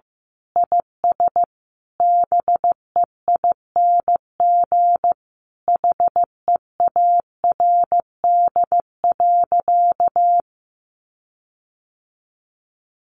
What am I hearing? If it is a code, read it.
Morse code, heard as a keyed tone, about 15 wpm, 700 Hz. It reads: IS BEING HEARD.
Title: Sentences from Top 1500 Words 15wpm